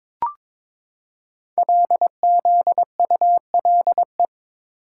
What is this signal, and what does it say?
Morse code, heard as a keyed tone, about 22 wpm, 700 Hz. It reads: LZULE
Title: Code Groups - 5 Letters - 22wpm